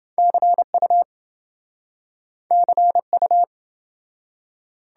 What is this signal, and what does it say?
Morse code, heard as a keyed tone, about 28 wpm, 700 Hz. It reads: CU CU